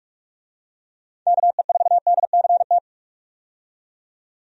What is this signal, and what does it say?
Morse code, heard as a keyed tone, about 48 wpm, 700 Hz. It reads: KE4DCT
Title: USA Call Signs - 6 Characters - 45wpm